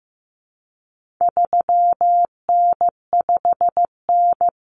Morse code, heard as a keyed tone, about 15 wpm, 700 Hz. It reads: 3N5N
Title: Common Cut Numbers - 4-Digit 15wpm